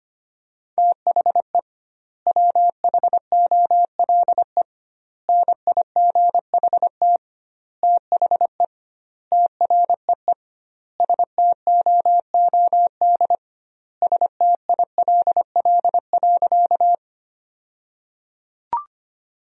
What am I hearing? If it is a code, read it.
Morse code, heard as a keyed tone, about 25 wpm, 700 Hz. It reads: THE WHOLE NIGHT THE TREE STOOD STILL.
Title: 2x Word-Spacing - Sentences from Top 500 Words - 25wpm